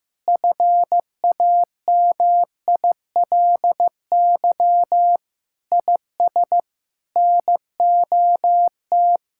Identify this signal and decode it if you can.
Morse code, heard as a keyed tone, about 15 wpm, 700 Hz. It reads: FAMILY IS NOT
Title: Sentences from Top 1000 Words - Review Part 6 - 15wpm